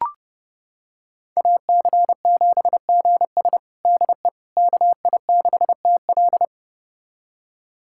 Morse code, heard as a keyed tone, about 30 wpm, 700 Hz. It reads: AC7GS DE KI6TL
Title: USA Callsign DE Callsign - 5 Characters - 30wpm